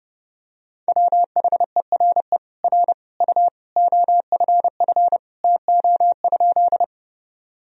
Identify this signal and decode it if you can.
Morse code, heard as a keyed tone, about 30 wpm, 700 Hz. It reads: WHERE R U OFF TO?
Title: Sentences from Top 200 Words and Abbreviations 30wpm